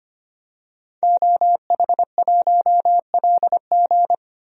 Morse code, heard as a keyed tone, about 25 wpm, 700 Hz. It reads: OH1LG
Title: DXCC-100 Call Signs - 5 Characters - 25wpm